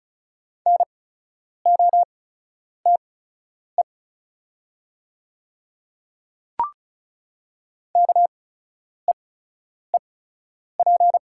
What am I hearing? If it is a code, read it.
Morse code, heard as a keyed tone, about 36 wpm, spaced as for 10 wpm, 700 Hz. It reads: NOTE KEEP
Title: 8x Character Spacing - Top 500 Words - 35wpm